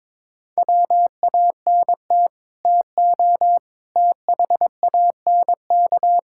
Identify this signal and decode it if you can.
Morse code, heard as a keyed tone, about 22 wpm, 700 Hz. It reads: WANT TO THANK
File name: Sets of 9 Words - 22wpm